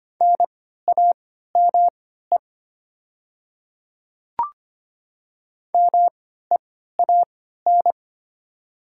Morse code, heard as a keyed tone, about 25 wpm, spaced as for 14 wpm, 700 Hz. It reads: NAME MEAN
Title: 3x Character Spacing - Top 300 Words - 25wpm